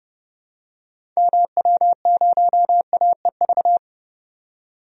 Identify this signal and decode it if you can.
Morse code, heard as a keyed tone, about 30 wpm, 700 Hz. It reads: MW0AEV
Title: DXCC-100 Call Signs - 6 Characters - 30wpm